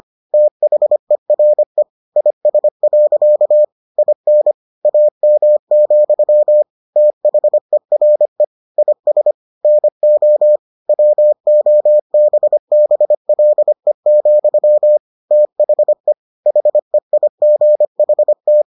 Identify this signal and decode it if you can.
Morse code, heard as a keyed tone, about 25 wpm, 600 Hz. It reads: THERE IS. IN AM, THERE IS NO WOBBLE, THE HEIGHT